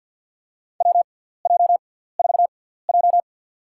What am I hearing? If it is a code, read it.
Morse code, heard as a keyed tone, about 53 wpm, spaced as for 17 wpm, 700 Hz. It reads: WJ4J
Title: USA Call Signs - 4 Characters - 6x Character Spacing - 50wpm